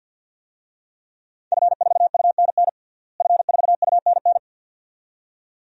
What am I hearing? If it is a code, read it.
Morse code, heard as a keyed tone, about 51 wpm, 700 Hz. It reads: F4UNN F4UNN
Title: DXCC-100 Call Signs - 5 Characters - Repeated 2x - 50wpm